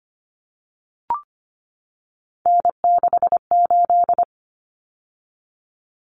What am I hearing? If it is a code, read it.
Morse code, heard as a keyed tone, about 25 wpm, 700 Hz. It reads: N68
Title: Common Cut Numbers - 3-Digit 25wpm